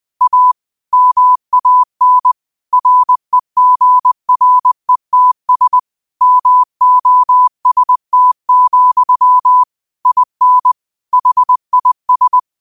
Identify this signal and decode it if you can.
Morse code, heard as a keyed tone, about 20 wpm, 1000 Hz. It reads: A MAN REGRETS MOST, IN HIS
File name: qotdfiles / 20wpm